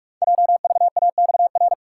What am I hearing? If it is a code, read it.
Morse code, heard as a keyed tone, about 44 wpm, 700 Hz. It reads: JVAXR
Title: Code Groups - 5 Letter:Numbers - 45wpm